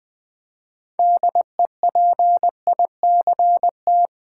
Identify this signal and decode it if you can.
Morse code, heard as a keyed tone, about 20 wpm, 700 Hz. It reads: DEPICT
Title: Build A Word - 3 Letters At A Time - 6-Letter Words - 20wpm